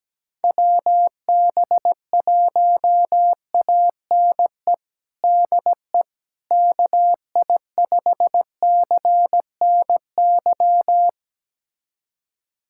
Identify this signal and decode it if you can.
Morse code, heard as a keyed tone, about 17 wpm, 700 Hz. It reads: WB1ANE DE KI5CNY
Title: USA Callsign DE Callsign - 6 Characters - 17wpm